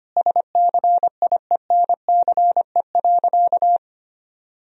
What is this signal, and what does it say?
Morse code, heard as a keyed tone, about 25 wpm, 700 Hz. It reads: SCIENCE.
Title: Sentences from Top 1000 Words - Review Part 4 - 25wpm